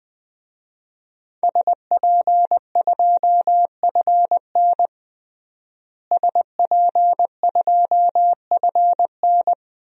Morse code, heard as a keyed tone, about 20 wpm, 700 Hz. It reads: SP2FN SP2FN
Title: DXCC-100 Call Signs - 5 Characters - Repeated 2x - 20wpm